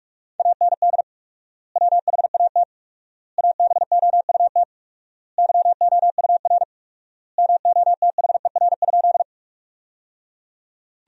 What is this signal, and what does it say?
Morse code, heard as a keyed tone, about 43 wpm, 700 Hz. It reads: AND WHAT ABOUT YOUR MOTHER?
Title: 4x Word-Spacing - Sentences from Top 200 Words - 45wpm